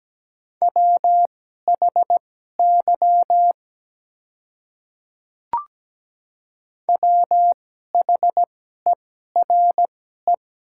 Morse code, heard as a keyed tone, about 17 wpm, spaced as for 12 wpm, 700 Hz. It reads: WHY WHERE
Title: 2x Character Spacing - Top 200 Words - 17wpm